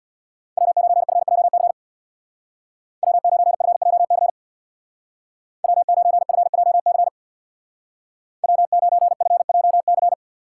Morse code, heard as a keyed tone, about 61 wpm, 700 Hz. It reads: W9FJC W9FJC W9FJC W9FJC
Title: Speed Racing - USA 1x3 Call Signs 50wpm